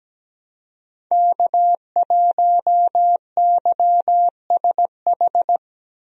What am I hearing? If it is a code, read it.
Morse code, heard as a keyed tone, about 17 wpm, 700 Hz. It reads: K1YSH
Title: USA Call Signs - 5 Characters - Repeated 2x - 17wpm